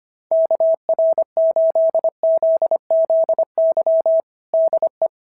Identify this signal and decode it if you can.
Morse code, heard as a keyed tone, about 23 wpm, 650 Hz. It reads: KR8ZZY DE
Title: SOTA QSO Practice 25wpm